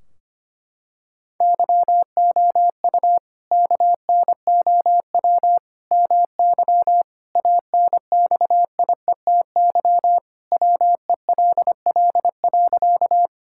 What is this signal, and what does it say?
Morse code, heard as a keyed tone, about 25 wpm, 700 Hz. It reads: YOU KNOW MY ANXIETY WELL.
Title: Sentences from Top 1600 Words 25wpm